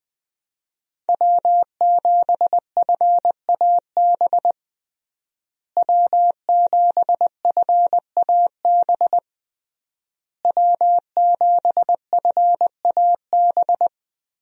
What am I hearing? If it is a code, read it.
Morse code, heard as a keyed tone, about 20 wpm, 700 Hz. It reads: W7FAB W7FAB W7FAB
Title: USA Call Signs - 5 Characters - Repeated 3x - 20wpm